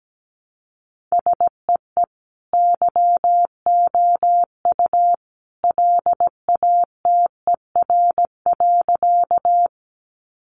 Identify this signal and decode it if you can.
Morse code, heard as a keyed tone, about 17 wpm, 700 Hz. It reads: SEE YOU LATER.
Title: Sentences from Top 700 Words 17wpm